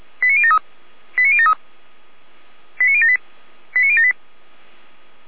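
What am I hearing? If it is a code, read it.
Five-tone selective call (ZVEI), heard as quick runs of five tones, tones of about 70 ms. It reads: 89073 89073 89078 89078
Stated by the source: ZVEI five-tone selective call received by SDR, from audio.f85360.feuerwehr-orlamuende.org.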